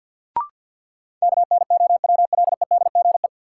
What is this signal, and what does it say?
Morse code, heard as a keyed tone, about 46 wpm, 700 Hz. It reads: KNOWLEDGE
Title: Sentences from Top 600 Words - New Word Drill - 50wpm